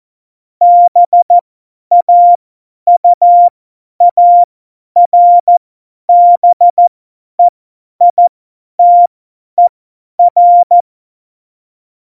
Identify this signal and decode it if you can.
Morse code, heard as a keyed tone, about 14 wpm, spaced as for 10 wpm, 700 Hz. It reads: BAUARBEITER
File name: Morsekurs / Witze / Witz 22. Oktober 2019, 50 F70